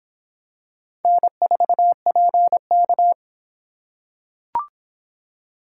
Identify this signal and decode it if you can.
Morse code, heard as a keyed tone, about 24 wpm, 700 Hz. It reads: N4PK